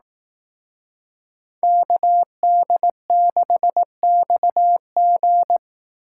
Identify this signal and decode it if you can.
Morse code, heard as a keyed tone, about 18 wpm, 700 Hz. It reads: KD6XG
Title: Speed Racing - USA 2x2 Call Signs 15wpm